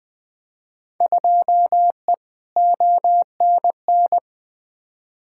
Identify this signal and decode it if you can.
Morse code, heard as a keyed tone, about 20 wpm, 700 Hz. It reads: 2E ONN
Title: Field Day Exchanges Encore - 20wpm